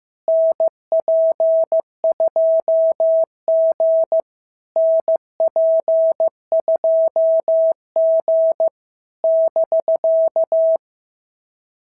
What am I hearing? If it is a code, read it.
Morse code, heard as a keyed tone, about 13 wpm, 650 Hz. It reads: NP2G NP2G <BK>
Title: SOTA QSO Practice 15wpm